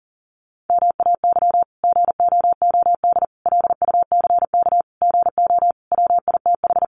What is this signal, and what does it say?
Morse code, heard as a keyed tone, about 40 wpm, 700 Hz. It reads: MAY GOOD LUCK GO WITH
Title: Sentences from Top 1500 Words 40wpm